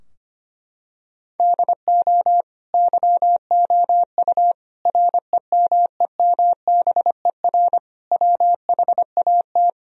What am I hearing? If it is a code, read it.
Morse code, heard as a keyed tone, about 25 wpm, 700 Hz. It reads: DO YOU REMEMBER WHAT
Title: Sentences from Top 1000 Words - Review Part 4 - 25wpm